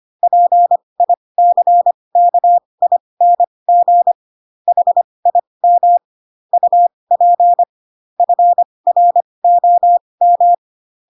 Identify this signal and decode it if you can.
Morse code, heard as a keyed tone, about 25 wpm, spaced as for 20 wpm, 700 Hz. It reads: PICKING HIM UP FROM
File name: Pinocchio / Pinocchio ch7 15 25 ik1hge